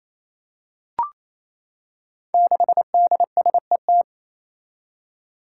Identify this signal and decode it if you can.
Morse code, heard as a keyed tone, about 28 wpm, 700 Hz. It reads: 6DSET
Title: Five Challenging Letter-Number Combinations 28wpm